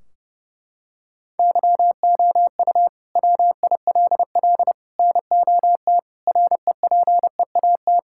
Morse code, heard as a keyed tone, about 30 wpm, 700 Hz. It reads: YOU WILL NOT REPEAT
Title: Sentences from Top 1600 Words 30wpm